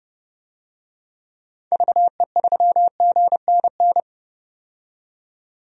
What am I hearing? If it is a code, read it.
Morse code, heard as a keyed tone, about 32 wpm, 700 Hz. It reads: VE3GNN
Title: DXCC-100 Call Signs - 6 Characters - 30wpm